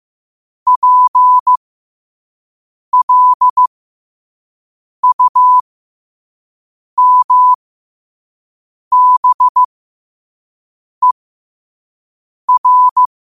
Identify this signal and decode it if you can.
Morse code, heard as a keyed tone, about 15 wpm, spaced as for 5 wpm, 1000 Hz. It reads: PLUMBER
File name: 5wpm